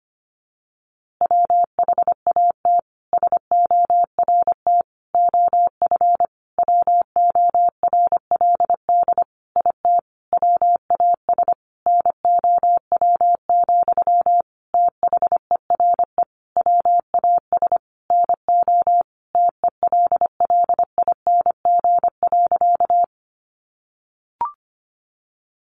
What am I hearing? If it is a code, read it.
Morse code, heard as a keyed tone, about 25 wpm, 700 Hz. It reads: WHAT SORT OF WORLD IT WAS NOW, THERE WAS NO TELLING.
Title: A Sound Of Thunder 25wpm